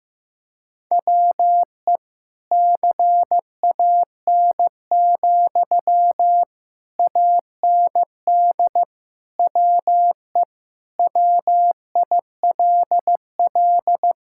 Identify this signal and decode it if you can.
Morse code, heard as a keyed tone, about 15 wpm, 700 Hz. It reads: WE CAN, AND WE WILL
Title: Sentences from Top 1000 Words - Review Part 6 - 15wpm